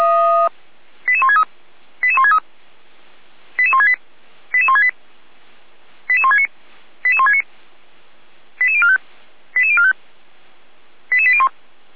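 Five-tone selective call (ZVEI), heard as quick runs of five tones, tones of about 70 ms. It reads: 88163 88163 88168 88168 88169 88169 88045 88045 88801
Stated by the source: ZVEI five-tone selective call 88365 received by SDR.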